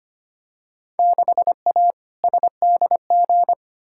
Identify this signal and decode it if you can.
Morse code, heard as a keyed tone, about 25 wpm, 700 Hz. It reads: 6A SDG
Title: Field Day Exchanges Encore - 25wpm